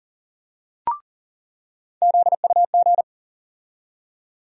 Morse code, heard as a keyed tone, about 40 wpm, 700 Hz. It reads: ZUG
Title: USA Call Sign Postfixes - 3 Letters 40wpm